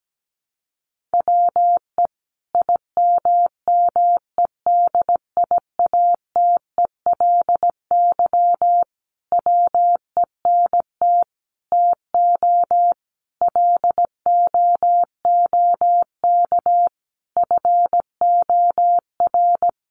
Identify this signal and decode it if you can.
Morse code, heard as a keyed tone, about 17 wpm, 700 Hz. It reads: WE IMMEDIATELY WENT TO LOOK FOR
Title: Sentences from Top 1100 Words 17wpm